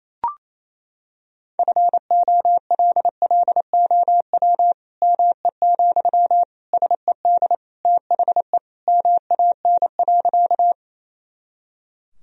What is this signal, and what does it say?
Morse code, heard as a keyed tone, about 27 wpm, 700 Hz. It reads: FOLLOW ME, SED THE MAN.
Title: Sentences from Top 200 Words and Abbreviations 28wpm